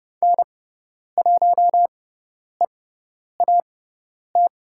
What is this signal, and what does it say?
Morse code, heard as a keyed tone, about 30 wpm, spaced as for 10 wpm, 700 Hz. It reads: N1EAT
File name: Lesson 16g - Letter c - Call Signs 30wpm